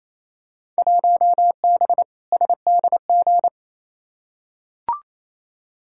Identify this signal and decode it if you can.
Morse code, heard as a keyed tone, about 28 wpm, 700 Hz. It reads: 1B SDG
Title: Field Day Exchanges Encore - 28wpm